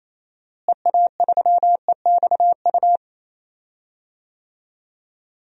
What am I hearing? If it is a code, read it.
Morse code, heard as a keyed tone, about 28 wpm, 700 Hz. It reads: EA3EXU
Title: DXCC-100 Call Signs - 6 Characters - Repeated 2x - 28wpm